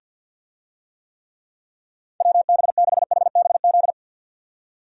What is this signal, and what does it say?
Morse code, heard as a keyed tone, about 52 wpm, 700 Hz. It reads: WB6HBZ
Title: USA Call Signs - 6 Characters - 50wpm